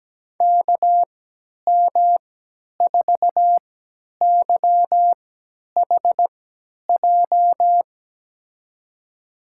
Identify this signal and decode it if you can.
Morse code, heard as a keyed tone, about 17 wpm, spaced as for 10 wpm, 700 Hz. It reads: KM4YHJ
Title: USA Call Signs - 3x Character Spacing - 17wpm